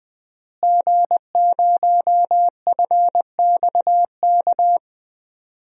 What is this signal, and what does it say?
Morse code, heard as a keyed tone, about 20 wpm, 700 Hz. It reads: G0FXK
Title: DXCC-100 Call Signs - 5 Characters - Repeated 2x - 20wpm